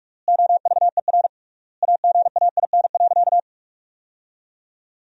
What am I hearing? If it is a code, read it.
Morse code, heard as a keyed tone, about 46 wpm, 700 Hz. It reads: OVER AGAIN.
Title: 3x Word-Spacing - Sentences from Top 500 Words - 45wpm